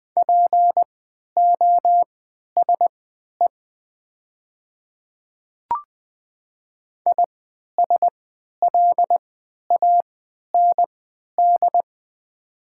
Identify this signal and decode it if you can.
Morse code, heard as a keyed tone, about 20 wpm, spaced as for 11 wpm, 700 Hz. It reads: POSE ISLAND
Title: 3x Character Spacing - Top 500 Words - 20wpm